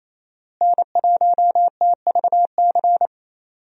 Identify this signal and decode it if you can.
Morse code, heard as a keyed tone, about 28 wpm, 700 Hz. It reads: N1TVC
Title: USA Call Signs - 5 Characters - Repeated 2x - 28wpm